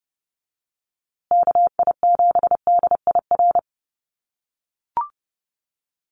Morse code, heard as a keyed tone, about 30 wpm, 700 Hz. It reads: KI7DIR
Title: USA Call Signs Repeated 3x 30wpm